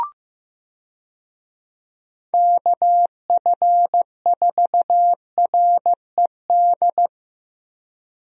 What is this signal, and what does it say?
Morse code, heard as a keyed tone, about 15 wpm, 700 Hz. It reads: KF4RED
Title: USA Call Signs - 6 Characters - Repeated 3x - 15wpm